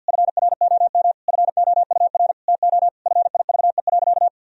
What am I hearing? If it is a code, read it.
Morse code, heard as a keyed tone, about 51 wpm, 700 Hz. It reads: FROM FOUR TO FIVE.
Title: Sentences from Top 1000 Words - Review Part 2 - 50wpm